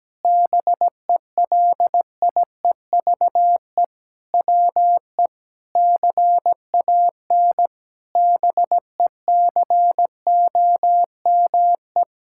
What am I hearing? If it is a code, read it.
Morse code, heard as a keyed tone, about 17 wpm, 700 Hz. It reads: BELIEVE WE CAN BECOME
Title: Sentences from Top 1000 Words - Review Part 6 - 17wpm